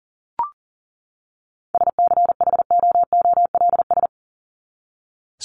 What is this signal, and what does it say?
Morse code, heard as a keyed tone, about 40 wpm, 700 Hz. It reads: SCHOOLS
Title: ES:S Suffix - Top 100 Words with Base Words 40wpm